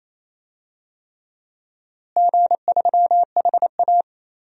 Morse code, heard as a keyed tone, about 28 wpm, 700 Hz. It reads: G3HA
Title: DXCC-100 Call Signs - 4 Characters - Repeated 3x - 28wpm